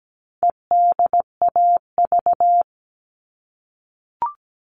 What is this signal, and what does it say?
Morse code, heard as a keyed tone, about 17 wpm, 700 Hz. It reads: EDAV
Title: All Cut Numbers - 4-Digit 17wpm